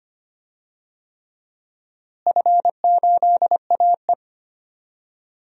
Morse code, heard as a keyed tone, about 25 wpm, 700 Hz. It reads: F8AE
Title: DXCC-100 Call Signs - 4 Characters - Repeated 3x - 25wpm